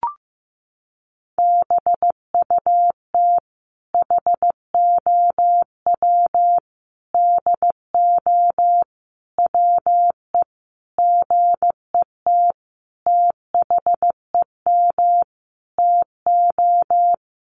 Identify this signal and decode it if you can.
Morse code, heard as a keyed tone, about 15 wpm, 700 Hz. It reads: BUT HOW DO WE GET THEM TO
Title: Sentences from Top 700 Words 15wpm